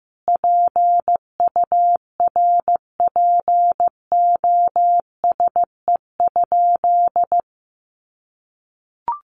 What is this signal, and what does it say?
Morse code, heard as a keyed tone, about 15 wpm, 700 Hz. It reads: PURPOSE?
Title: Sentences from Top 1500 Words 15wpm